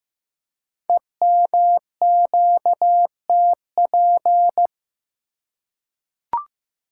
Morse code, heard as a keyed tone, about 15 wpm, 700 Hz. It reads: EMQTP
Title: Code Groups - 5 Letters - 15wpm